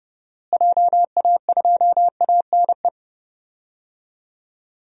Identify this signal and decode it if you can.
Morse code, heard as a keyed tone, about 31 wpm, 700 Hz. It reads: JA2ANE
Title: DXCC-100 Call Signs - 6 Characters - Repeated 3x - 30wpm